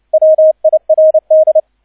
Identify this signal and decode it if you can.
Morse code, heard as a keyed tone, about 29 wpm, 630 Hz, with radio noise. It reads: WIRD
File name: 002 Zweiter Streich 30wpm